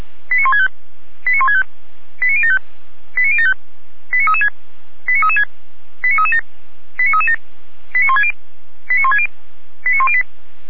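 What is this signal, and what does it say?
Five-tone selective call (ZVEI), heard as quick runs of five tones, tones of about 70 ms. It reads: 89165 89165 89075 89075 89336 89336 89337 89337 89160 89160 89108
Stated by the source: ZVEI five-tone selective call received by SDR, address 89150.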